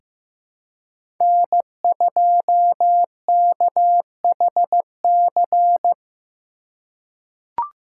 Morse code, heard as a keyed tone, about 15 wpm, 700 Hz. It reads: N2KHC
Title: USA Call Signs - 5 Characters - 3x Character Spacing - 15wpm